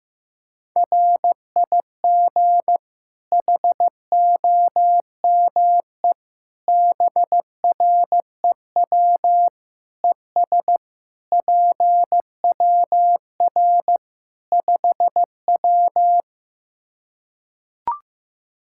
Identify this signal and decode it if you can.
Morse code, heard as a keyed tone, about 15 wpm, 700 Hz. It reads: RIG HOME BREW ES PWR 5W
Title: K1USN SST QSO Practice 15wpm